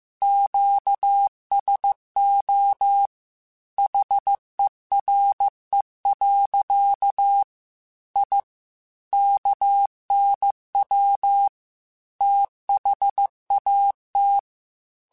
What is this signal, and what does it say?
Morse code, heard as a keyed tone, about 15 wpm, 800 Hz. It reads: QSO HERE. I KNW THAT
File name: QSOs / QSOs-mp3s / WD0BGP 2@13